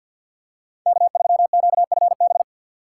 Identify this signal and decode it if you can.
Morse code, heard as a keyed tone, about 51 wpm, 700 Hz. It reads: K3QFB
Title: USA Call Signs - 5 Characters - 4x Character Spacing - 50wpm